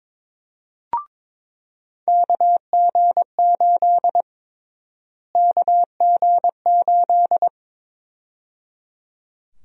Morse code, heard as a keyed tone, about 22 wpm, 700 Hz. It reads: KG8 KG8
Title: USA Call Sign Prefixes - Repeated 2x - 22wpm